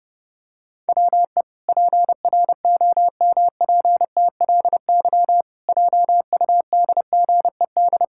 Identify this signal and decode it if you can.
Morse code, heard as a keyed tone, about 30 wpm, 700 Hz. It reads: WE PROMPTLY JUDGED
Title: Pangram Challenge v2 - 30wpm